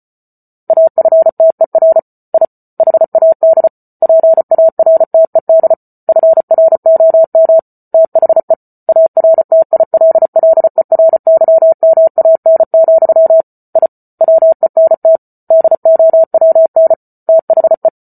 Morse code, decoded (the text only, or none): AFTER I HAD PARTED FROM THE ARTILLERYMAN, I WENT DOWN THE